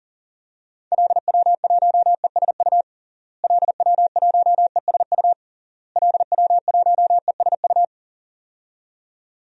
Morse code, heard as a keyed tone, about 40 wpm, 700 Hz. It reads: LW1ESU LW1ESU LW1ESU
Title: DXCC-100 Call Signs - 6 Characters - Repeated 3x - 40wpm